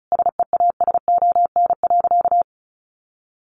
Morse code, heard as a keyed tone, about 34 wpm, 700 Hz. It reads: SEASON.
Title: Sentences from Top 1400 Words 35wpm